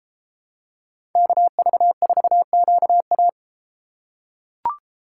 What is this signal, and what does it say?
Morse code, heard as a keyed tone, about 32 wpm, 700 Hz. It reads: KV4QA